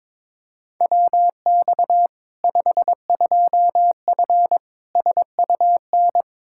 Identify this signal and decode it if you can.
Morse code, heard as a keyed tone, about 22 wpm, 700 Hz. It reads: WX 52F SUN
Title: Practice QSO - Exchange 2 - Info - 22wpm